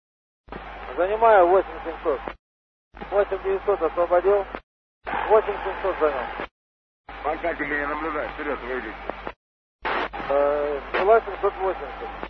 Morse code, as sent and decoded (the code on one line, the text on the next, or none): none